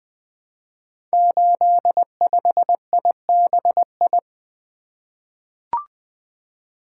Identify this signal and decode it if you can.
Morse code, heard as a keyed tone, about 20 wpm, 700 Hz. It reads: 85IBI
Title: Code Groups - 5 Letter:Numbers - 20wpm